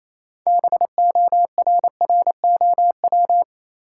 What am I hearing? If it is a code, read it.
Morse code, heard as a keyed tone, about 27 wpm, 700 Hz. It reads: BORROW
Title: Sentences from Top 1500 Words - New Word Drill Alt - 28wpm